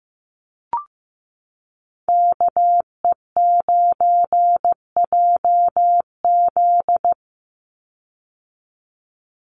Morse code, heard as a keyed tone, about 15 wpm, 700 Hz. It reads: KE9JZ
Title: USA Call Signs 15wpm